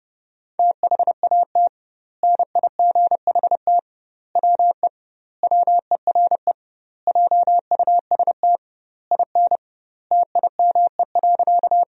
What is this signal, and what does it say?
Morse code, heard as a keyed tone, about 30 wpm, 700 Hz. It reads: THAT NIGHT WE WERE JUST IN TIME.
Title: 2x Word-Spacing - Sentences from Top 300 Words - 30wpm